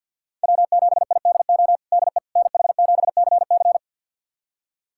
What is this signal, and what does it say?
Morse code, heard as a keyed tone, about 51 wpm, 700 Hz. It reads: W7IDO DE NH7CC
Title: USA Callsign DE Callsign - 5 Characters - 50wpm